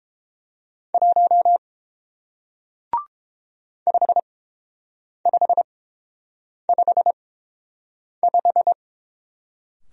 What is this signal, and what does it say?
Morse code, heard as a keyed tone, about 30 wpm, spaced as for 7 wpm, 700 Hz. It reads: 1 5555